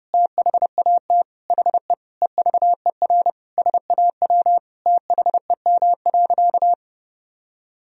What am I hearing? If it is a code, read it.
Morse code, heard as a keyed tone, about 30 wpm, 700 Hz. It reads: THAT HE EVER SAW THEM.
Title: Sentences from Top 1000 Words - Review Part 2 - 30wpm